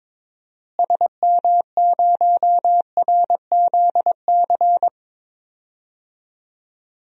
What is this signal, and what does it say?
Morse code, heard as a keyed tone, about 22 wpm, 700 Hz. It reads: SM0RZC
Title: DXCC-100 Call Signs - 6 Characters - Repeated 2x - 22wpm